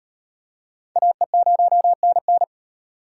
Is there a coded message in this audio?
Yes: Morse code from 1.0 to 2.4 s, heard as a keyed tone, about 38 wpm, 700 Hz. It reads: AE0NN